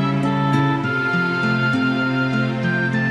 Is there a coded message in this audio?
No.